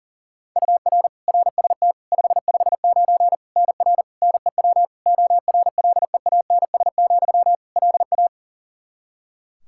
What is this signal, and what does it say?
Morse code, heard as a keyed tone, about 37 wpm, 700 Hz. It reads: UR RST 559 NR NEW ORLEANS, LA
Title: K1USN SST QSO Practice 40wpm